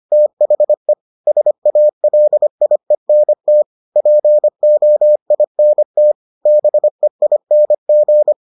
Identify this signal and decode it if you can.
Morse code, heard as a keyed tone, about 25 wpm, 600 Hz. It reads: THE SALIENT POINT BEING